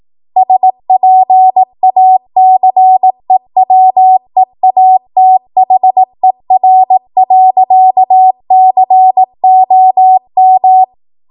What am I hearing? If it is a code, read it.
Morse code, heard as a keyed tone, about 18 wpm, 750 Hz. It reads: SPACEWEATHER.COM
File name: Morse / 18WPM